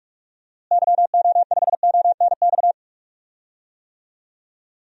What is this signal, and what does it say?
Morse code, heard as a keyed tone, about 46 wpm, 700 Hz. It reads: YO5ONX
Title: DXCC-100 Call Signs - 6 Characters - Repeated 2x - 45wpm